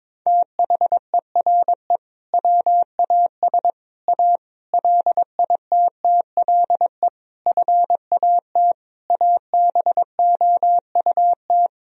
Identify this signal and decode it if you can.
Morse code, heard as a keyed tone, about 22 wpm, 700 Hz. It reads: THERE WAS A LITTLE FAT ABOUT
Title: Sentences from Top 1000 Words - Review Part 4 - 22wpm